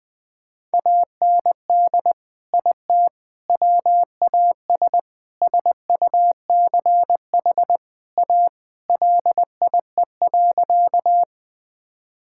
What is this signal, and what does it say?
Morse code, heard as a keyed tone, about 21 wpm, 700 Hz. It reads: AND IT WAS SUCH A LIE.
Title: Sentences from Top 1600 Words 20wpm